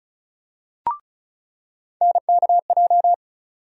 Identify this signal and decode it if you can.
Morse code, heard as a keyed tone, about 36 wpm, 700 Hz. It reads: NKJ
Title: USA Call Sign Postfixes - 3 Letters 35wpm